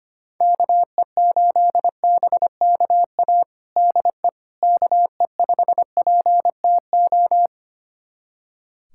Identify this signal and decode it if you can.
Morse code, heard as a keyed tone, about 25 wpm, 700 Hz. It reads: KE8BKA DE KE5PTO
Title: USA Callsign DE Callsign - 6 Characters - 25wpm